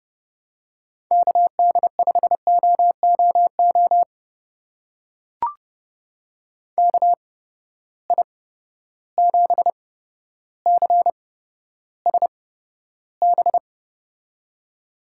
Morse code, heard as a keyed tone, about 30 wpm, spaced as for 13 wpm, 700 Hz. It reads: KD5OOO KI7CSB
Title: USA Call Signs - 6 Characters - 8x Character Spacing - 30wpm